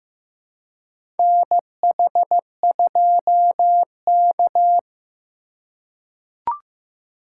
Morse code, heard as a keyed tone, about 15 wpm, 700 Hz. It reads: NH2K
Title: USA Call Signs - 2x Character Spacing - 15wpm